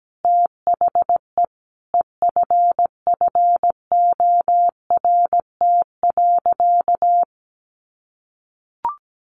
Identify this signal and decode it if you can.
Morse code, heard as a keyed tone, about 17 wpm, 700 Hz. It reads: THE EFFORT.
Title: Sentences from Top 1200 Words 17wpm